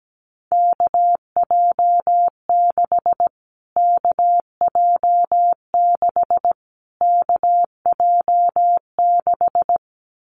Morse code, heard as a keyed tone, about 17 wpm, 700 Hz. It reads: KJ6 KJ6 KJ6
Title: Callsign Prefixes for Top 25 DXCC - Repeated 3x - 17wpm